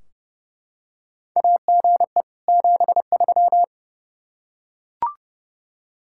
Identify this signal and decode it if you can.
Morse code, heard as a keyed tone, about 30 wpm, 700 Hz. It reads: AGE 73
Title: K1USN SST QSO Practice 30wpm